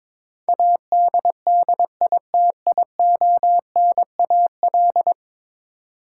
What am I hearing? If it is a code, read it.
Morse code, heard as a keyed tone, about 21 wpm, 700 Hz. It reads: ADDITIONAL
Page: morse-code-ninja-practice-22wpm